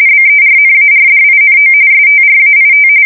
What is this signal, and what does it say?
RTTY, about 45.45 baud. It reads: BERNARDINO STRAIT
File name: fsk170 / battleshipnj